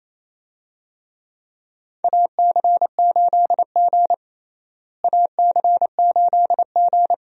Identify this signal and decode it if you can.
Morse code, heard as a keyed tone, about 28 wpm, 700 Hz. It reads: AC8G AC8G
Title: USA Call Signs - 4 Characters - Repeated 2x - 28wpm